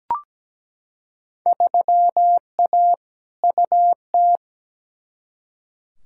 Morse code, heard as a keyed tone, about 17 wpm, 700 Hz. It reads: 3A UT
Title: Field Day Exchanges - 17wpm